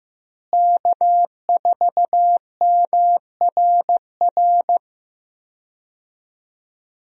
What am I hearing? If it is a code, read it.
Morse code, heard as a keyed tone, about 15 wpm, 700 Hz. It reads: K4MRR
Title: USA Call Signs - 5 Characters - Repeated 3x - 15wpm